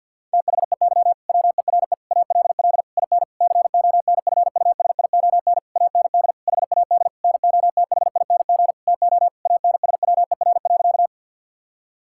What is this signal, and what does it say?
Morse code, heard as a keyed tone, about 50 wpm, 700 Hz. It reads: THEY WERE ALL IN CONFUSION AND HAD NOTHING TO ANSWER.
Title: Sentences from Top 1600 Words 50wpm